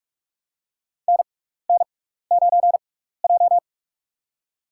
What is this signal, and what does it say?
Morse code, heard as a keyed tone, about 46 wpm, spaced as for 16 wpm, 690 Hz. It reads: NN9J
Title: USA Call Signs - 6x Character Spacing - 45wpm